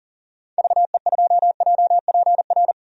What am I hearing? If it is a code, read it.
Morse code, heard as a keyed tone, about 38 wpm, 700 Hz. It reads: VE2JPR
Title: DXCC-100 Call Signs - 6 Characters - 40wpm